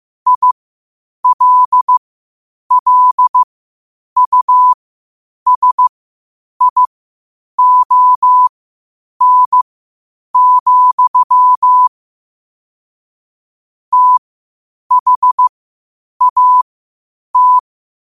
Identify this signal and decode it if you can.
Morse code, heard as a keyed tone, about 15 wpm, spaced as for 8 wpm, 1000 Hz. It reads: ILLUSION, THAT
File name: qotdfiles / 8wpm